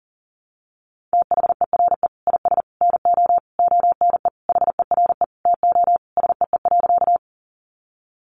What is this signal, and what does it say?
Morse code, heard as a keyed tone, about 40 wpm, 700 Hz. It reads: THERE IS NO ONE HERE TO SEE.
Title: Sentences from Top 200 Words 40wpm